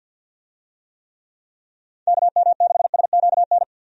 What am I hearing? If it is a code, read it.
Morse code, heard as a keyed tone, about 51 wpm, 700 Hz. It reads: KM6SQN
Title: USA Call Signs - 6 Characters - 50wpm